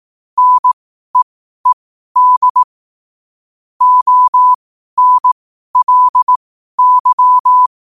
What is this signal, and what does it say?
Morse code, heard as a keyed tone, about 18 wpm, spaced as for 12 wpm, 1000 Hz. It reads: NEED ONLY
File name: qotdfiles / 12wpm